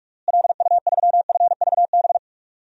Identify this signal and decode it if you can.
Morse code, heard as a keyed tone, about 46 wpm, 700 Hz. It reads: LU3FVB
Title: DXCC-100 Call Signs - 6 Characters - Repeated 3x - 45wpm